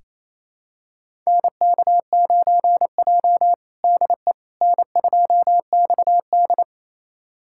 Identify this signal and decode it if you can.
Morse code, heard as a keyed tone, about 27 wpm, 700 Hz. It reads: NK9J DE N2XD
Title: USA Callsign DE Callsign - 4 Characters - 28wpm